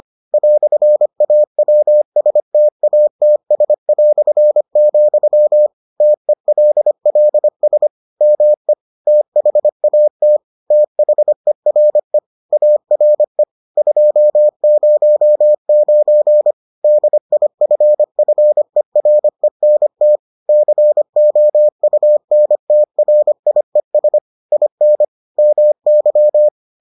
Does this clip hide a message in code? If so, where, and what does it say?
Yes: Morse code from 0.3 to 26.5 s, heard as a keyed tone, about 25 wpm, 600 Hz. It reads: "AWSTATS", TELLS ME THAT THERE ARE 209 DIFFERENT COUNTRIES IN MY